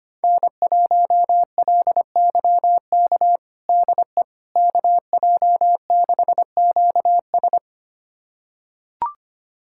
Morse code, heard as a keyed tone, about 25 wpm, 700 Hz. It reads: N1LYK DE KJ6QS
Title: USA Callsign DE Callsign - 5 Characters - 25wpm